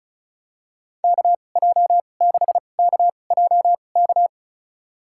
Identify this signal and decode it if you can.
Morse code, heard as a keyed tone, about 34 wpm, spaced as for 25 wpm, 700 Hz. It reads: KJ6KJK
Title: USA Call Signs - 2x Character Spacing - 35wpm